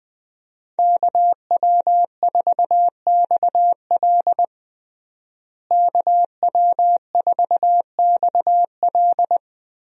Morse code, heard as a keyed tone, about 20 wpm, 700 Hz. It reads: KW4XL KW4XL
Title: USA Call Signs - 5 Characters - Repeated 3x - 20wpm